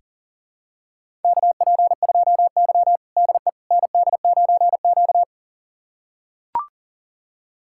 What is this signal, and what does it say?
Morse code, heard as a keyed tone, about 40 wpm, 700 Hz. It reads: KP2Y DE ND9Q
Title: USA Callsign DE Callsign - 4 Characters - 40wpm